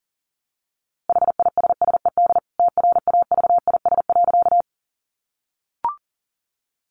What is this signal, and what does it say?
Morse code, heard as a keyed tone, about 40 wpm, 700 Hz. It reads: HISSED TRAVIS.
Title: A Sound Of Thunder 40wpm